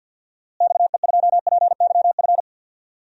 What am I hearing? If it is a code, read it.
Morse code, heard as a keyed tone, about 47 wpm, 700 Hz. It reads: XE2PYF
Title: DXCC-100 Call Signs - 6 Characters - Repeated 2x - 50wpm